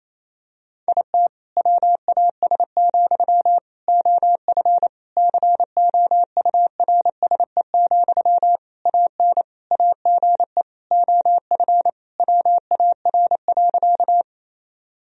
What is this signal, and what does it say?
Morse code, heard as a keyed tone, about 28 wpm, 700 Hz. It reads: IT WAS, OF COURSE, AN AGE OF WAR.
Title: Sentences from Top 1000 Words - Review Part 3 - 28wpm